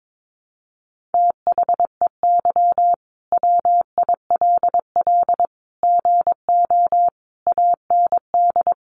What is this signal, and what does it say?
Morse code, heard as a keyed tone, about 22 wpm, 700 Hz. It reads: THEY WILL GO AND
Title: Sentences from Top 1100 Words 22wpm